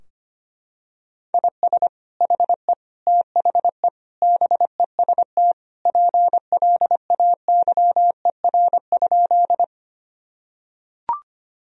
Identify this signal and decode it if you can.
Morse code, heard as a keyed tone, about 25 wpm, 700 Hz. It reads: IS HE THE BEST PLAYER?
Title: Sentences from Top 1000 Words - Review Part 4 - 25wpm